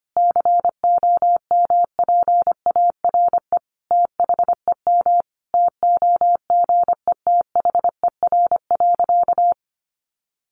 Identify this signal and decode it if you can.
Morse code, heard as a keyed tone, about 25 wpm, 700 Hz. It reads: COMPARE THEM TOGETHER.
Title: Sentences from Top 1300 Words 25wpm